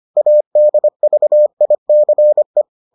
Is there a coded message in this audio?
Yes: Morse code from 0.2 to 2.6 s, heard as a keyed tone, about 25 wpm, 600 Hz. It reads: ADVICE